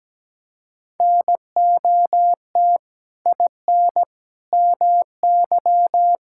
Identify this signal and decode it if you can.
Morse code, heard as a keyed tone, about 17 wpm, 700 Hz. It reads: NOT IN MY
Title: Sentences from Top 1000 Words - Review Part 5 - 17wpm